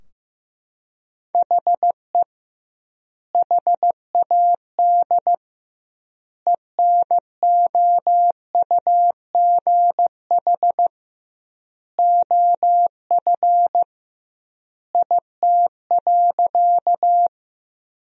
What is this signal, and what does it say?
Morse code, heard as a keyed tone, about 15 wpm, 700 Hz. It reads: HE HAD ENOUGH OF IT.
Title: 2x Word-Spacing - Sentences from Top 400 Words - 15wpm